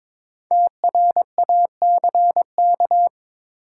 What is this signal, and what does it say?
Morse code, heard as a keyed tone, about 22 wpm, 700 Hz. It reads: TRACK
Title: Sentences from Top 1100 Words - New Word Drill - 22wpm